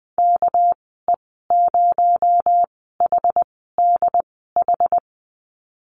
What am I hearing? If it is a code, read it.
Morse code, heard as a keyed tone, about 20 wpm, spaced as for 15 wpm, 700 Hz. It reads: KE0HDH
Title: 231 Lesson 37g - Letter x - Call Signs 20wpm